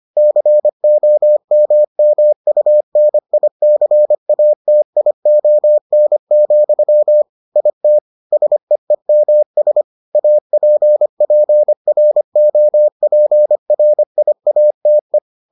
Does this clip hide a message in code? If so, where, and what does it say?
Yes: Morse code from 0.2 to 15.2 s, heard as a keyed tone, about 25 wpm, 600 Hz. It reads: COMMUNICATION, IT SEEMS APPROPRIATE